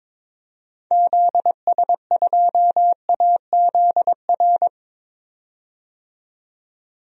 Morse code, heard as a keyed tone, about 21 wpm, 700 Hz. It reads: ZS2AZR
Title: DXCC-100 Call Signs - 6 Characters - 22wpm